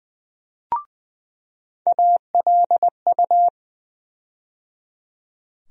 Morse code, heard as a keyed tone, about 20 wpm, 700 Hz. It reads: ALU
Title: USA Call Sign Postfixes - 3 Letters 20wpm